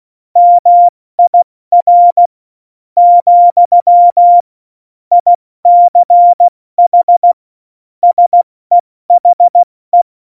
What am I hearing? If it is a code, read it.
Morse code, heard as a keyed tone, about 16 wpm, 700 Hz. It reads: MIR , ICH SEHE
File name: Morsekurs / Witze / Witz 23. Oktober 2018, 70 F80